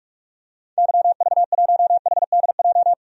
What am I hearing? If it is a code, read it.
Morse code, heard as a keyed tone, about 46 wpm, 700 Hz. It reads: YV1HDJ
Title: DXCC-100 Call Signs - 6 Characters - 45wpm